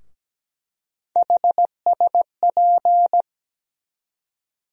Morse code, heard as a keyed tone, about 17 wpm, 700 Hz. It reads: HSP
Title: USA Call Sign Postfixes - 3 Letters 17wpm